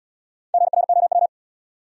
Morse code, heard as a keyed tone, about 76 wpm, 700 Hz. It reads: GROW